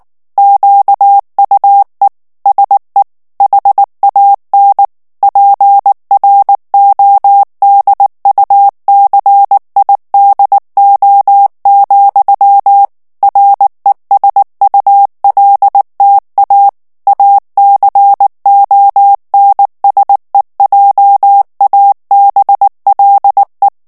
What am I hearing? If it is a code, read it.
Morse code, heard as a keyed tone, about 19 wpm, 800 Hz. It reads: QUE SE HAN PRODUCIDO, RESULTA ACONSEJABLE
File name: morse mp3 / ESPAGNOL TEXTE EN CLAIR 2 VITESSE 18 MOTS MINUTES